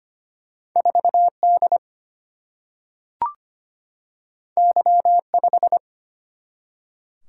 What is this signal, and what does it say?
Morse code, heard as a keyed tone, about 25 wpm, 700 Hz. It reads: 4D Y5
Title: Single-Letter Single-Number 25wpm